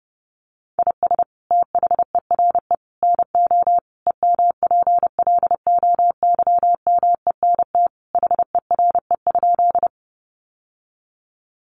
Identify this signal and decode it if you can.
Morse code, heard as a keyed tone, about 30 wpm, 700 Hz. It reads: IS THERE NO EMPLOYMENT HERE?
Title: Sentences from Top 1500 Words 30wpm